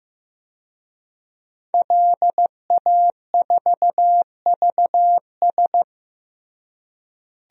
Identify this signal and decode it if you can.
Morse code, heard as a keyed tone, about 15 wpm, 700 Hz. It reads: LA4VS
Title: DXCC-100 Call Signs - 5 Characters - Repeated 2x - 15wpm